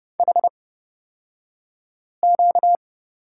Morse code, heard as a keyed tone, about 30 wpm, spaced as for 5 wpm, 700 Hz. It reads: HQ
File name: Lesson 38d - Letter q - Two Characters 30wpm